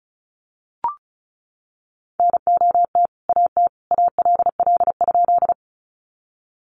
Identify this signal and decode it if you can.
Morse code, heard as a keyed tone, about 34 wpm, 700 Hz. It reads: NOT AT ALL?
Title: Sentences from Top 600 Words 35wpm